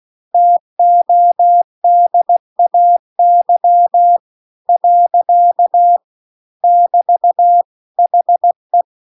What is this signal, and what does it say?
Morse code, heard as a keyed tone, about 16 wpm, 700 Hz. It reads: TODAY . <BT> HE